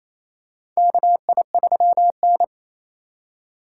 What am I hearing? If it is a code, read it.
Morse code, heard as a keyed tone, about 28 wpm, 700 Hz. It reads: KI3N
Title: USA Call Signs - 4x Character Spacing - 28wpm